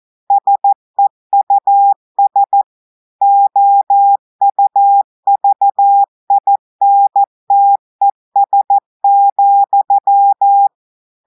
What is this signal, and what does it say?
Morse code, heard as a keyed tone, about 14 wpm, 800 Hz. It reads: SEUS OUVINTES,
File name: audios / portugais / portugais texte en clair 2 vitesse 14 mots minutes